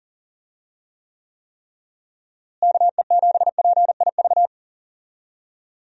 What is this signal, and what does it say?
Morse code, heard as a keyed tone, about 40 wpm, 700 Hz. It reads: KE7PIV
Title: USA Call Signs - 6 Characters - Repeated 3x - 40wpm